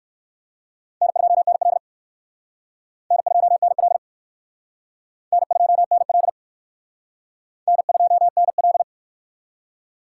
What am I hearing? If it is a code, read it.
Morse code, heard as a keyed tone, about 59 wpm, 700 Hz. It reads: N2NL N2NL N2NL N2NL